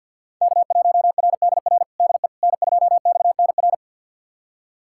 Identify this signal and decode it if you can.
Morse code, heard as a keyed tone, about 51 wpm, 700 Hz. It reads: K1RDR DE N2XNR
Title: USA Callsign DE Callsign - 5 Characters - 50wpm